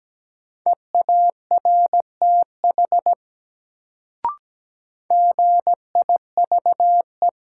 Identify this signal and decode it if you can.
Morse code, heard as a keyed tone, about 17 wpm, 700 Hz. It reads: EARTH GIVE
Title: Sentences from Top 200 Words - New Word Drill - 17wpm